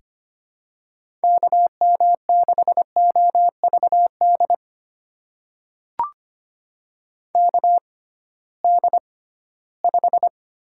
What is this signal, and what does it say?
Morse code, heard as a keyed tone, about 25 wpm, spaced as for 16 wpm, 700 Hz. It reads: KM6OVD KD5
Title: USA Call Signs - 6 Characters - 6x Character Spacing - 25wpm